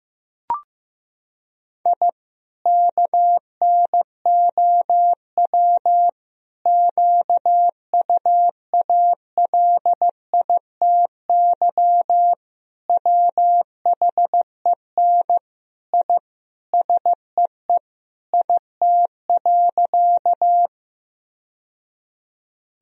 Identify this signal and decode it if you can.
Morse code, heard as a keyed tone, about 16 wpm, 700 Hz. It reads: I KNOW QUALITY WHEN I SEE IT.
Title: Sentences from Top 1000 Words - Review Part 4 - 15wpm